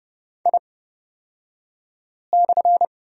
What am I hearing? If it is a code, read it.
Morse code, heard as a keyed tone, about 30 wpm, spaced as for 5 wpm, 700 Hz. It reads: I/
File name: Lesson 37d - Letter x - Two Characters 30wpm